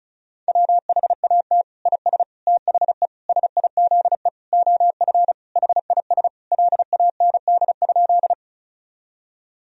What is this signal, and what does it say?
Morse code, heard as a keyed tone, about 34 wpm, 700 Hz. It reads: WHAT IS THE SIZE OF HIS LAND?
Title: Sentences from Top 1000 Words - Review Part 2 - 35wpm